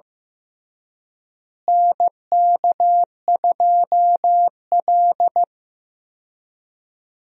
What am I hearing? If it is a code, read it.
Morse code, heard as a keyed tone, about 15 wpm, 700 Hz. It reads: NK2L
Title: USA Call Signs - 4 Characters - Repeated 3x - 15wpm